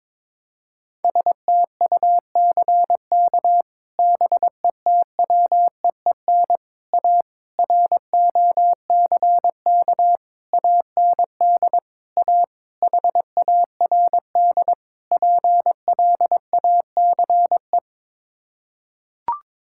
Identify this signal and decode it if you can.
Morse code, heard as a keyed tone, about 22 wpm, 700 Hz. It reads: STUCK BETWEEN A ROCK AND A HARD PLACE
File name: practice / 22wpm / Sets of 8 Words - 22wpm